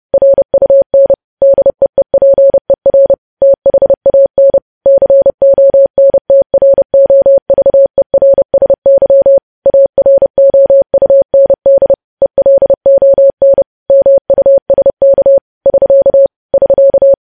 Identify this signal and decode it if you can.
Morse code, heard as a keyed tone, about 30 wpm, 550 Hz. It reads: RUN DEEPER THAN CONTROVERSY AROUND ELON MUSK <SK> <SK>
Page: news headlines in morse code at 30 wpm